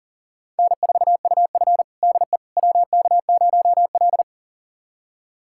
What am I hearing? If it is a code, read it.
Morse code, heard as a keyed tone, about 40 wpm, 700 Hz. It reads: N4UF DE WK0L
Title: USA Callsign DE Callsign - 4 Characters - 40wpm